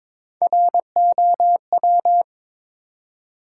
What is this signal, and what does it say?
Morse code, heard as a keyed tone, about 22 wpm, 700 Hz. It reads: ROW
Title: Three Letter Words - 22wpm